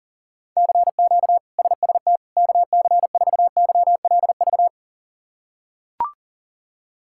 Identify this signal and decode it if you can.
Morse code, heard as a keyed tone, about 40 wpm, 700 Hz. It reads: CQ SST KC4YLV
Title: K1USN SST QSO Practice - 40wpm